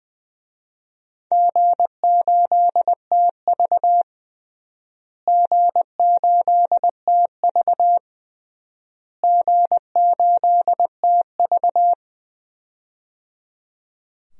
Morse code, heard as a keyed tone, about 20 wpm, 700 Hz. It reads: G8TV G8TV G8TV
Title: DXCC-100 Call Signs - 4 Characters - Repeated 3x - 20wpm